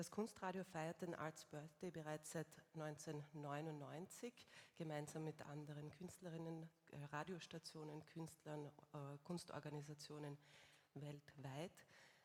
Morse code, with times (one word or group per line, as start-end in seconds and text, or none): none